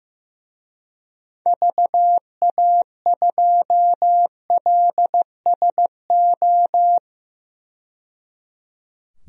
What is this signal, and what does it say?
Morse code, heard as a keyed tone, about 15 wpm, 700 Hz. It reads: VA2LSO
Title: DXCC-100 Call Signs - 6 Characters - 15wpm